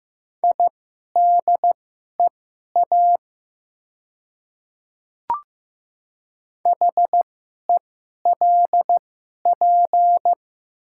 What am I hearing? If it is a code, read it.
Morse code, heard as a keyed tone, about 15 wpm, spaced as for 11 wpm, 690 Hz. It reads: IDEA HELP
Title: 2x Character Spacing - Top 500 Words - 15wpm